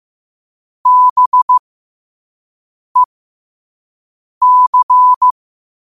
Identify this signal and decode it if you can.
Morse code, heard as a keyed tone, about 15 wpm, spaced as for 5 wpm, 1000 Hz. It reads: BEC
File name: qotdfiles / 5wpm